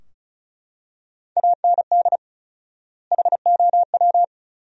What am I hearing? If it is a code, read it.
Morse code, heard as a keyed tone, about 34 wpm, 700 Hz. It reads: AND HOW
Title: 4x Word-Spacing - Sentences from Top 200 Words - 35wpm